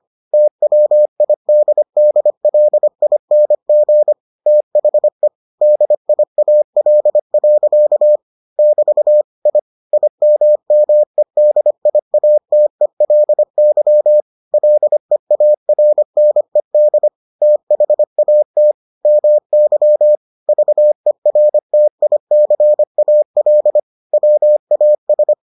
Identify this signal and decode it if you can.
Morse code, heard as a keyed tone, about 25 wpm, 600 Hz. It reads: TWIDDLING THE DIAL. <BT> I IMMEDIATELY LEARNED THAT MY VERTICAL WAS